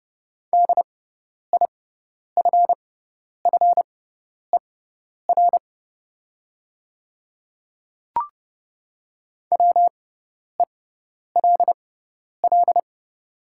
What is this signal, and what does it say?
Morse code, heard as a keyed tone, about 30 wpm, spaced as for 10 wpm, 690 Hz. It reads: DIFFER WELL
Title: 6x Character Spacing - Top 200 Words - 30wpm